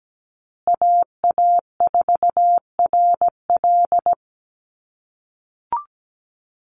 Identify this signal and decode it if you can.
Morse code, heard as a keyed tone, about 17 wpm, 700 Hz. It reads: AA4RL
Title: USA Call Signs Repeated 2x 17wpm